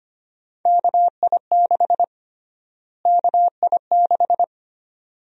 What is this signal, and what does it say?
Morse code, heard as a keyed tone, about 25 wpm, 700 Hz. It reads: KI6 KI6
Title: USA Call Sign Prefixes - Repeated 2x - 25wpm